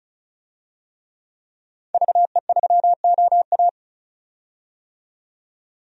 Morse code, heard as a keyed tone, about 36 wpm, 700 Hz. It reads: VE3OA
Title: DXCC-100 Call Signs - 5 Characters - 35wpm